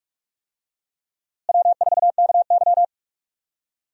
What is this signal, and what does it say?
Morse code, heard as a keyed tone, about 43 wpm, 700 Hz. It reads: W4KY
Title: USA Call Signs - 4 Characters - Repeated 2x - 45wpm